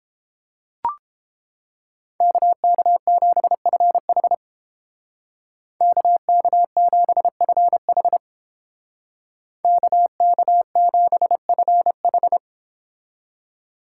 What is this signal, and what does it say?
Morse code, heard as a keyed tone, about 30 wpm, 700 Hz. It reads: KK7FH KK7FH KK7FH